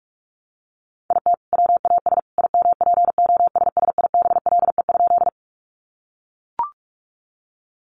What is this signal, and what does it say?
Morse code, heard as a keyed tone, about 44 wpm, 700 Hz. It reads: IT WAS IMPOSSIBLE?
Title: Sentences from Top 1100 Words 45wpm